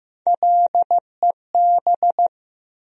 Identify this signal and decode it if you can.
Morse code, heard as a keyed tone, about 15 wpm, 700 Hz. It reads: LEB
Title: USA Call Sign Postfixes - 3 Letters 15wpm